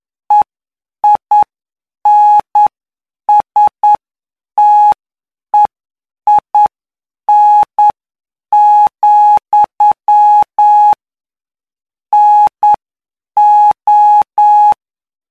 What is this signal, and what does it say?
Morse code, heard as a keyed tone, about 9 wpm, spaced as for 7 wpm, 840 Hz. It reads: EINSTEIN, NO
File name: cw / py2lcd / 17-PY2LCD-Li��o 26